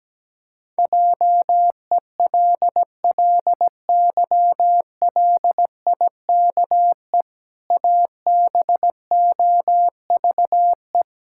Morse code, heard as a keyed tone, about 17 wpm, 700 Hz. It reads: JELLYLIKE ABOVE
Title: Pangram Challenge v2 - 17wpm